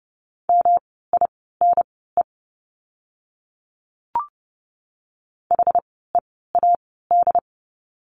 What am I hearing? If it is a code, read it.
Morse code, heard as a keyed tone, about 30 wpm, spaced as for 17 wpm, 700 Hz. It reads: MINE HEAD
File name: course / 30:10wpm / 108 Lesson 18f - Letter m - Words 30wpm